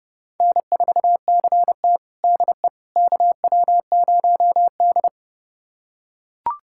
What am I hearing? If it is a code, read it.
Morse code, heard as a keyed tone, about 31 wpm, 700 Hz. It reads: N4CT DE KW0D
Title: USA Callsign DE Callsign - 4 Characters - 30wpm